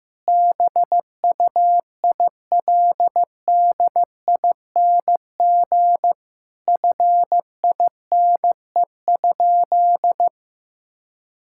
Morse code, heard as a keyed tone, about 16 wpm, 700 Hz. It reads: BUILDING FINE?
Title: Sentences from Top 1000 Words - Review Part 5 - 15wpm